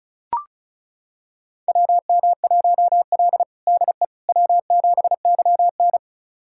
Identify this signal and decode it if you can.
Morse code, heard as a keyed tone, about 36 wpm, 700 Hz. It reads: WM1L DE W7YN
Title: USA Callsign DE Callsign - 4 Characters - 35wpm